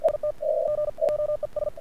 Morse code, heard as a keyed tone, about 51 wpm, spaced as for 41 wpm, 1200 Hz, with noise. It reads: IT GOES
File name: archived / CRSnet